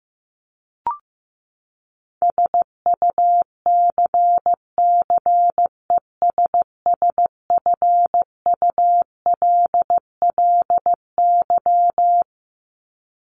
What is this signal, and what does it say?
Morse code, heard as a keyed tone, about 15 wpm, 700 Hz. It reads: SUCCESSFULLY
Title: LY Suffix - Top 100 Words with Base Words 15wpm